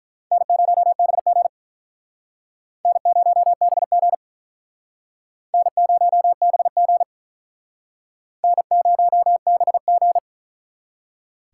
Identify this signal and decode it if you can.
Morse code, heard as a keyed tone, about 45 wpm, 700 Hz. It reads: N0BG N0BG N0BG N0BG